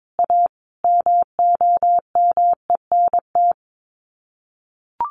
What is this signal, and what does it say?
Morse code, heard as a keyed tone, about 22 wpm, 700 Hz. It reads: A MOMENT
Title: Sets of 5 Words Encore 22wpm